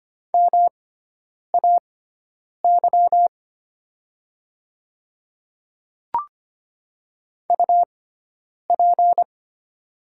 Morse code, heard as a keyed tone, about 25 wpm, spaced as for 9 wpm, 700 Hz. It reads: MAY UP
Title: 6x Character Spacing - Top 100 Words - 25wpm